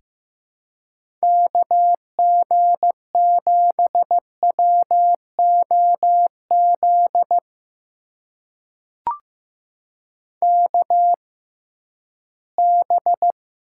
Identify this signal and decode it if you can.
Morse code, heard as a keyed tone, about 15 wpm, spaced as for 11 wpm, 700 Hz. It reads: KG7WOZ KB
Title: USA Call Signs - 6 Characters - 6x Character Spacing - 15wpm